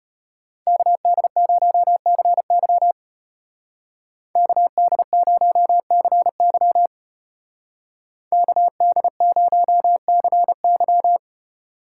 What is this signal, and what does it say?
Morse code, heard as a keyed tone, about 32 wpm, 700 Hz. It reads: KD0CY KD0CY KD0CY